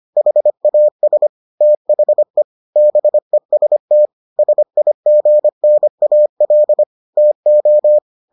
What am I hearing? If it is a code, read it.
Morse code, heard as a keyed tone, about 25 wpm, 600 Hz. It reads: HAS THE BEST SIGNAL TO